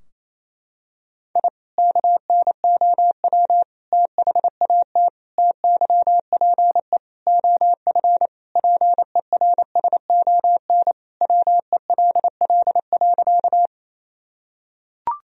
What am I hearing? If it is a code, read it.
Morse code, heard as a keyed tone, about 28 wpm, 700 Hz. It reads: I KNOW THAT TYPE OF PERSON WELL.
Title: Sentences from Top 1000 Words - Review Part 3 - 28wpm